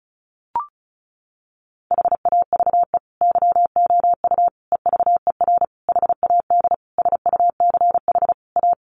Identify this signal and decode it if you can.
Morse code, heard as a keyed tone, about 34 wpm, 700 Hz. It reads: HAVE YOU EVER HAD SUCH A
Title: Sentences from Top 1200 Words 35wpm